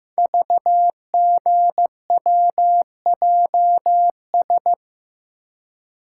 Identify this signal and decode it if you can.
Morse code, heard as a keyed tone, about 15 wpm, 700 Hz. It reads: VGWJS
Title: Code Groups - 5 Letters - 15wpm